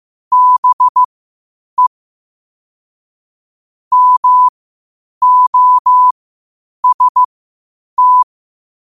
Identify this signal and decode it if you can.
Morse code, heard as a keyed tone, about 15 wpm, spaced as for 8 wpm, 1000 Hz. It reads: BE MOST
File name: qotdfiles / 8wpm